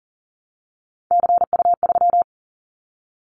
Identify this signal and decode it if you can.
Morse code, heard as a keyed tone, about 40 wpm, 700 Hz. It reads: CU3
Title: Callsign Prefixes for Top 100 DXCC - 40wpm